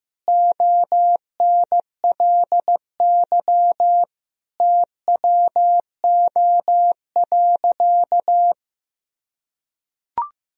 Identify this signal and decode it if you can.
Morse code, heard as a keyed tone, about 15 wpm, 700 Hz. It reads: ONLY TWO.
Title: Sentences from Top 200 Words and Abbreviations 15wpm